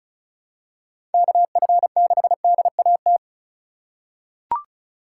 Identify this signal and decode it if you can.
Morse code, heard as a keyed tone, about 36 wpm, 700 Hz. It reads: KF6DAT
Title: USA Call Signs - 6 Characters - 2x Character Spacing - 35wpm